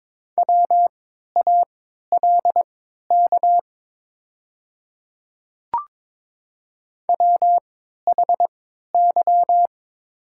Morse code, heard as a keyed tone, about 22 wpm, spaced as for 13 wpm, 700 Hz. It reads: WALK WHY
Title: 3x Character Spacing - Top 300 Words - 22wpm